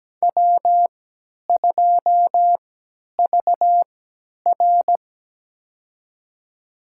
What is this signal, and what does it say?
Morse code, heard as a keyed tone, about 17 wpm, spaced as for 10 wpm, 700 Hz. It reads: W2VR
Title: USA Call Signs - 4 Characters - 3x Character Spacing - 17wpm